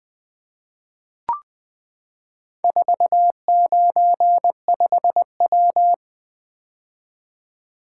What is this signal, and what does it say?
Morse code, heard as a keyed tone, about 20 wpm, 700 Hz. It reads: 495W
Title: Four Letter-Number Combinations - 20wpm